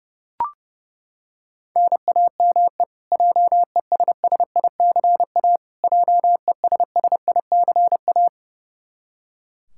Morse code, heard as a keyed tone, about 30 wpm, 700 Hz. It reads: NAME JESSICA JESSICA
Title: Practice QSO - Exchange 1 - Beginning - 30wpm